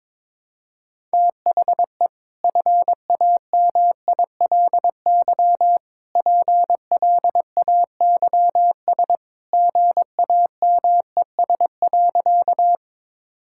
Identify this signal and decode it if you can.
Morse code, heard as a keyed tone, about 21 wpm, 700 Hz. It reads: THE FAMILY PLAYS GAMES.
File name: practice / 22wpm / Easy Sentences with Hints - 22wpm